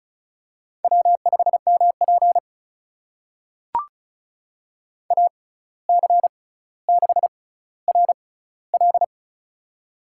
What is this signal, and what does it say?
Morse code, heard as a keyed tone, about 36 wpm, spaced as for 17 wpm, 700 Hz. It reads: W5MP AC6RL
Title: USA Call Signs - 6x Character Spacing - 35wpm